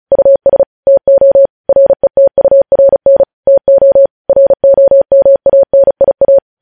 Morse code, heard as a keyed tone, about 36 wpm, 550 Hz. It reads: US TO RETURN TO ROMANIA
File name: podcast / News headlines-1x-35x35